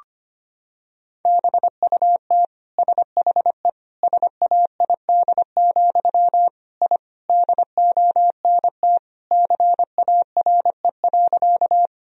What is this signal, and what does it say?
Morse code, heard as a keyed tone, about 25 wpm, 700 Hz. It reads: BUT SHE SAID, I DONT CARE.
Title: Sentences from Top 1000 Words - Review Part 1 - 25wpm